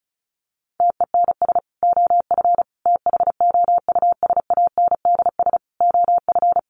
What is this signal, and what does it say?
Morse code, heard as a keyed tone, about 34 wpm, 700 Hz. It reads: TENS OF THOUSANDS OF
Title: Sets of 5 Words Encore2 35wpm